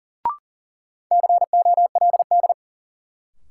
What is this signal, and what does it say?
Morse code, heard as a keyed tone, about 40 wpm, 700 Hz. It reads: COLD
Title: Sets of 5 Words - New Word Drill - 40wpm